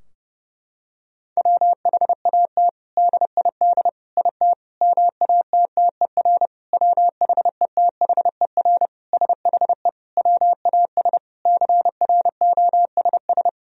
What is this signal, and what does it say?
Morse code, heard as a keyed tone, about 30 wpm, 700 Hz. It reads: WHAT DID IT MATTER WHETHER SHE WAS CROSS
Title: Sentences from Top 1000 Words - Review Part 3 - 30wpm